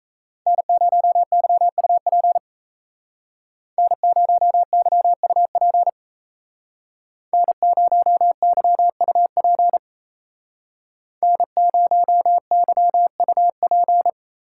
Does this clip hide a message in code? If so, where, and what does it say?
Yes: Morse code from 0.5 to 14.1 s, heard as a keyed tone, about 36 wpm, 700 Hz. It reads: N0YUP N0YUP N0YUP N0YUP